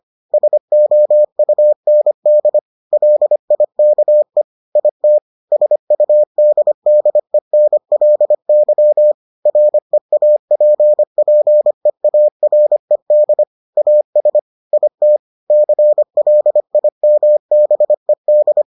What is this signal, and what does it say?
Morse code, heard as a keyed tone, about 25 wpm, 600 Hz. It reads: SOUND LIKE IT SUDDENLY REAPPEARED AS IT CLIMBED